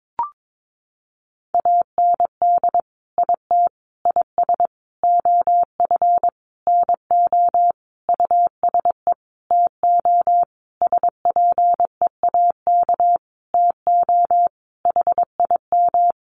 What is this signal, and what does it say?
Morse code, heard as a keyed tone, about 22 wpm, 700 Hz. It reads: AND IT IS OF NO USE TO SPEAK TO HIM
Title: Sentences from Top 1100 Words 22wpm